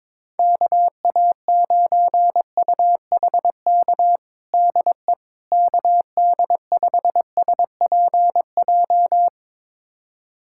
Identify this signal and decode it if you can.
Morse code, heard as a keyed tone, about 22 wpm, 700 Hz. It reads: KA9UHK DE KD5SPJ
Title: USA Callsign DE Callsign - 6 Characters - 22wpm